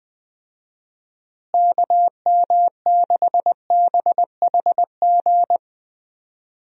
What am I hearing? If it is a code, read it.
Morse code, heard as a keyed tone, about 20 wpm, 700 Hz. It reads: KM6BHG
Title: USA Call Signs - 6 Characters - Repeated 3x - 20wpm